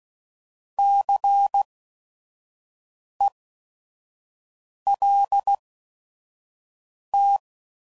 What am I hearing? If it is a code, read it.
Morse code, heard as a keyed tone, about 16 wpm, spaced as for 5 wpm, 800 Hz. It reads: CELT
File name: Prefixes / By First Letter / Cs